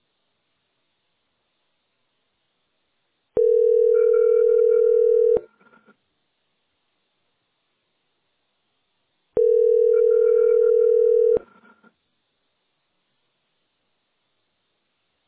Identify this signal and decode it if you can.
Ringback tone (North America), 440 + 480 Hz.